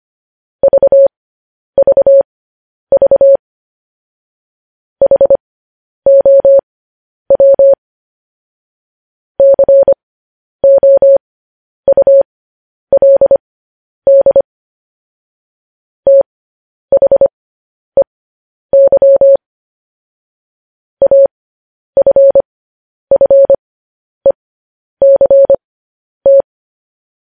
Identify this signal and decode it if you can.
Morse code, heard as a keyed tone, about 25 wpm, spaced as for 10 wpm, 550 Hz. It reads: VVV HOW COULD THEY AFFECT
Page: news headlines in morse code at 10 wpm